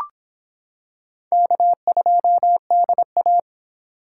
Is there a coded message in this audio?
Yes: Morse code from 1.3 to 3.4 s, heard as a keyed tone, about 24 wpm, 700 Hz. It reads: K2DA